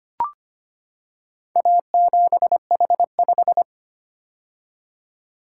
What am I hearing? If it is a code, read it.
Morse code, heard as a keyed tone, about 25 wpm, 700 Hz. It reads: A7H5
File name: practice / 25wpm / Four Letter-Number Combinations - 25wpm